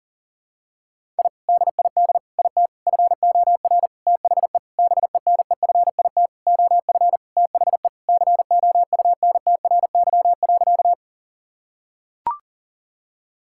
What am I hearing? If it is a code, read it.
Morse code, heard as a keyed tone, about 40 wpm, 700 Hz. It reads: I DID IT FOR THE BENEFIT OF THE COUNTRY.
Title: Sentences from Top 1000 Words - Review Part 6 - 40wpm